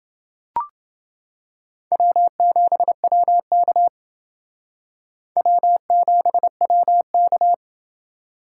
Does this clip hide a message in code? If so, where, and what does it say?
Yes: Morse code from 1.9 to 7.5 s, heard as a keyed tone, about 28 wpm, 700 Hz. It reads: W7WK W7WK